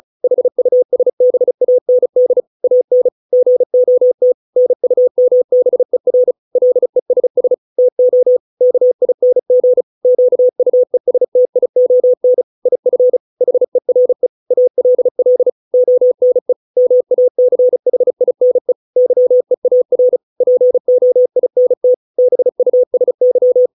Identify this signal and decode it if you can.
Morse code, heard as a keyed tone, about 34 wpm, 500 Hz. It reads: HUSBAND AN GOT NUMBER LESS TO KING QUESTION IF HERE ALL ONE MACHINE YEAR POINT BUSY
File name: 632 words / 632 Words 35WPM